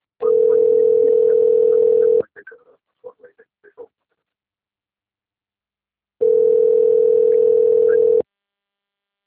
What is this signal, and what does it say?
Ringback tone (North America), 440 + 480 Hz, with speech.
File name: rg-1600-unknown- - - .72306